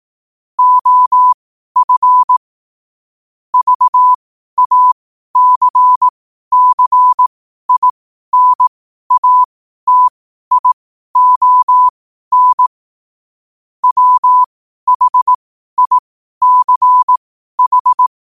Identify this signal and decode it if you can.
Morse code, heard as a keyed tone, about 18 wpm, spaced as for 12 wpm, 1000 Hz. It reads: OF VACCINATION WHICH
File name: qotdfiles / 12wpm